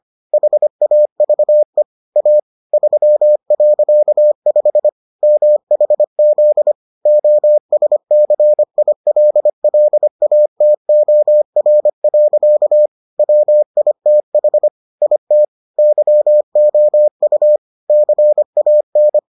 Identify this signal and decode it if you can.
Morse code, heard as a keyed tone, about 25 wpm, 600 Hz. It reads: HAVE A 3.5 MHZ OSCILLATOR. WITH IT YOU CAN